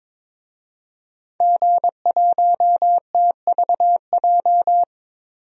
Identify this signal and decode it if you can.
Morse code, heard as a keyed tone, about 22 wpm, 700 Hz. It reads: G1TVJ
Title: DXCC-100 Call Signs - 5 Characters - 22wpm